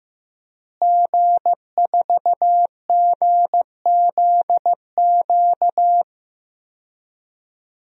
Morse code, heard as a keyed tone, about 15 wpm, 700 Hz. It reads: G4GZQ
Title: DXCC-100 Call Signs - 5 Characters - Repeated 3x - 15wpm